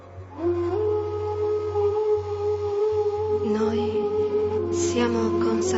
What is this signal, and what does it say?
Congestion tone (425 Hz).